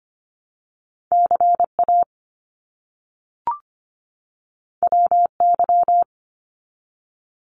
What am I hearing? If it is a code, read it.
Morse code, heard as a keyed tone, about 25 wpm, 700 Hz. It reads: CA WY
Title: US States 25wpm